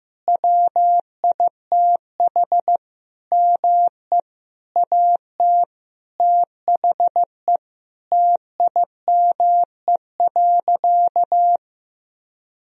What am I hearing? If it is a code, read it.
Morse code, heard as a keyed tone, about 16 wpm, 700 Hz. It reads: WITH ME AT THE TIME.
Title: Sentences from Top 1000 Words - Review Part 4 - 15wpm